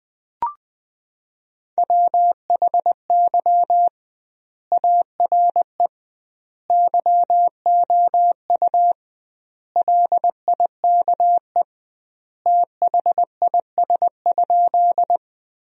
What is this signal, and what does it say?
Morse code, heard as a keyed tone, about 20 wpm, 700 Hz. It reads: WHY ARE YOU LIKE THIS?
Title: 2x Word-Spacing - Sentences from Top 200 Words - 20wpm